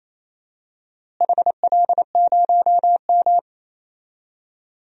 Morse code, heard as a keyed tone, about 28 wpm, 700 Hz. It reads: HL0M
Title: DXCC-100 Call Signs - 4 Characters - Repeated 3x - 28wpm